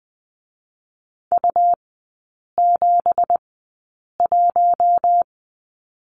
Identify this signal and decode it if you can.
Morse code, heard as a keyed tone, about 20 wpm, spaced as for 8 wpm, 690 Hz. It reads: U71
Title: 166 Lesson 27e - Number 7 - Three Characters 20wpm